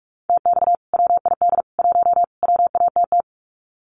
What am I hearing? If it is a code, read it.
Morse code, heard as a keyed tone, about 44 wpm, 700 Hz. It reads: TX WID 1 WATT
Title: Common QSO Phrases 45wpm